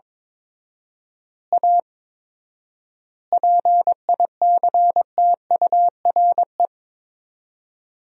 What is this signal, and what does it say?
Morse code, heard as a keyed tone, about 22 wpm, 700 Hz. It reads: A PICTURE
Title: 4x Word-Spacing - Sentences from Top 500 Words - 22wpm